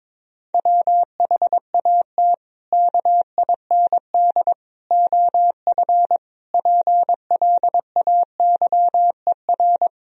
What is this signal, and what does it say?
Morse code, heard as a keyed tone, about 22 wpm, 700 Hz. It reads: WHAT KIND OF PLAYER
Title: Sentences from Top 1000 Words - Review Part 4 - 22wpm